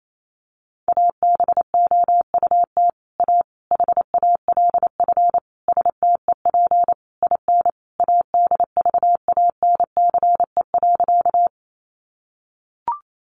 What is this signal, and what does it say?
Morse code, heard as a keyed tone, about 28 wpm, 700 Hz. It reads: ABOUT A HALF STEP IN ADVANCE.
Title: Sentences from Top 1400 Words 28wpm